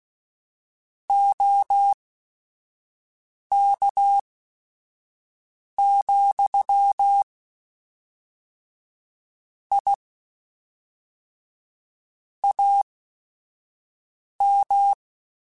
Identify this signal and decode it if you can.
Morse code, heard as a keyed tone, about 16 wpm, spaced as for 5 wpm, 800 Hz. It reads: OK, I AM